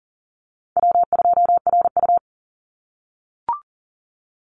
Morse code, heard as a keyed tone, about 40 wpm, 700 Hz. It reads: W2RU
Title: Easy Call Signs - 40wpm